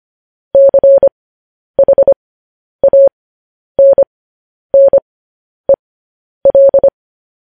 Morse code, heard as a keyed tone, about 25 wpm, spaced as for 10 wpm, 550 Hz. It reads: CHANNEL